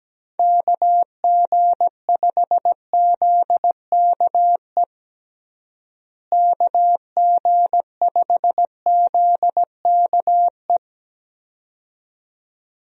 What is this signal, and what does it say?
Morse code, heard as a keyed tone, about 17 wpm, 700 Hz. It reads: KG5ZKE KG5ZKE
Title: USA Call Signs - 6 Characters - Repeated 2x - 17wpm